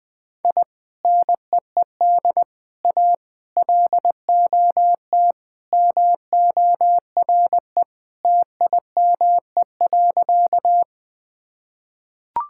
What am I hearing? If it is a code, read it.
Morse code, heard as a keyed tone, about 21 wpm, 700 Hz. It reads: I NEED A LOT MORE TIME.
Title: Sentences with 3-Word Phrases - 20wpm